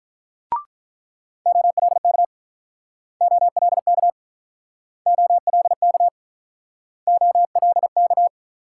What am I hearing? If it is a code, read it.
Morse code, heard as a keyed tone, about 44 wpm, 700 Hz. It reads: OLK OLK OLK OLK